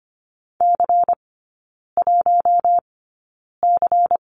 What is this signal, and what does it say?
Morse code, heard as a keyed tone, about 25 wpm, spaced as for 9 wpm, 700 Hz. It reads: C1C
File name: course / 25:10wpm / 093 Lesson 16e - Letter c - Three Characters 25wpm